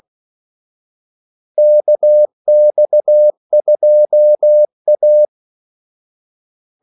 Morse code, heard as a keyed tone, about 16 wpm, 600 Hz. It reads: KX2A
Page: Kaldesignaler 16 wpm